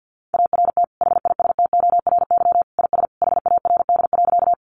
Morse code, heard as a keyed tone, about 50 wpm, 700 Hz. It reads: ART HISTORY IS HARD.
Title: Sentences from Top 600 Words 50wpm